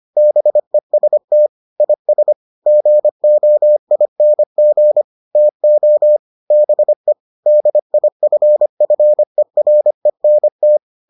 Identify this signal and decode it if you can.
Morse code, heard as a keyed tone, about 25 wpm, 600 Hz. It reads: BEST IS GOING TO BE DIFFERENT